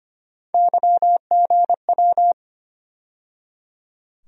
Morse code, heard as a keyed tone, about 25 wpm, 700 Hz. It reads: YGW
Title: USA Call Sign Postfixes - 3 Letters 25wpm